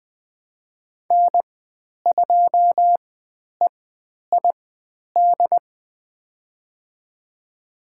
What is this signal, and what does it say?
Morse code, heard as a keyed tone, about 20 wpm, spaced as for 10 wpm, 700 Hz. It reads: N2EID
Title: 095 Lesson 16g - Letter c - Call Signs 20wpm